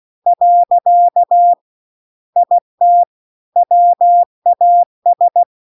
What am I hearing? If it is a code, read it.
Morse code, heard as a keyed tone, about 16 wpm, 700 Hz. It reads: . IT WAS